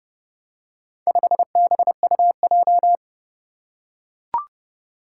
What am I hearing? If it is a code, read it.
Morse code, heard as a keyed tone, about 32 wpm, 700 Hz. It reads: 5BUJ
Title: Four Challenging Letter-Number Combinations 30wpm